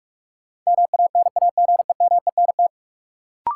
Morse code, heard as a keyed tone, about 46 wpm, 700 Hz. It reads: MANAGEMENT
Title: Sentences from Top 1000 Words - Review Part 4 - New Word Drill - 45wpm